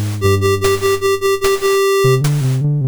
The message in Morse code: none